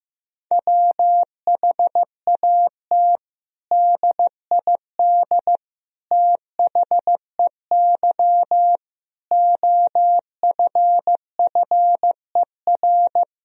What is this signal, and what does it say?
Morse code, heard as a keyed tone, about 15 wpm, 700 Hz. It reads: WHAT DID THEY OFFER
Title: Sentences from Top 1000 Words - Review Part 4 - 15wpm